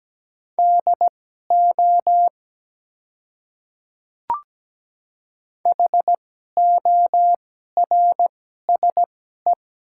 Morse code, heard as a keyed tone, about 17 wpm, spaced as for 12 wpm, 700 Hz. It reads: DO HORSE
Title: 2x Character Spacing - Top 300 Words - 17wpm